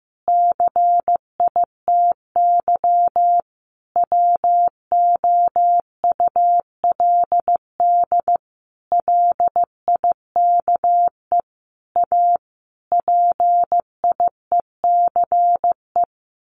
Morse code, heard as a keyed tone, about 15 wpm, 700 Hz. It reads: CITY WOULD LIKE A PIECE
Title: Sentences from Top 1100 Words 15wpm